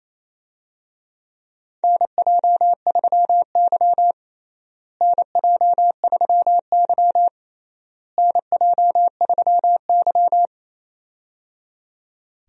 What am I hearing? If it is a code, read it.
Morse code, heard as a keyed tone, about 28 wpm, 700 Hz. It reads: NJ3Y NJ3Y NJ3Y
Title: USA Call Signs - 4 Characters - Repeated 3x - 28wpm